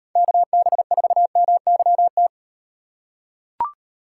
Morse code, heard as a keyed tone, about 37 wpm, 700 Hz. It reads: KB4MYT